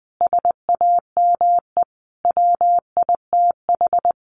Morse code, heard as a keyed tone, about 20 wpm, 700 Hz. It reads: SAME WITH
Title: Sentences from Top 300 Words 20wpm